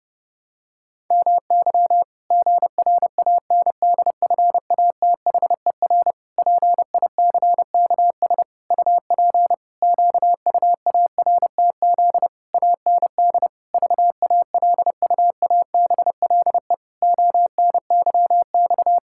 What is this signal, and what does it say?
Morse code, heard as a keyed tone, about 30 wpm, 700 Hz. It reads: MY GRANDFATHER PICKS UP QUARTZ AND VALUABLE ONYX
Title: Pangram Challenge v2 - 30wpm